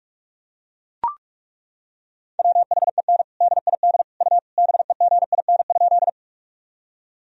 Morse code, heard as a keyed tone, about 41 wpm, 700 Hz. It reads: WHEN DID U BEGIN?
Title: Sentences from Top 300 Words and Abbreviations 45wpm